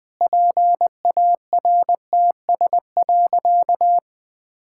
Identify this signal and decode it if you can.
Morse code, heard as a keyed tone, about 20 wpm, 700 Hz. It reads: PARTS.
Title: Sentences with 6-Word Phrases - Spoken Beginning - 20wpm